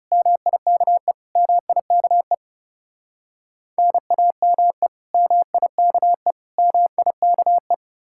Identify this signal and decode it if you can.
Morse code, heard as a keyed tone, about 37 wpm, 700 Hz. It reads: MIKE MIKE NAME MIKE MIKE
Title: Speed-Racing Common QSO Phrases - 30wpm